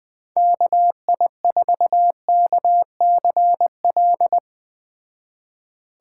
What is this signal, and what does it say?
Morse code, heard as a keyed tone, about 20 wpm, 700 Hz. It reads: KI4KCL
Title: Speed Racing - USA Call Signs 17wpm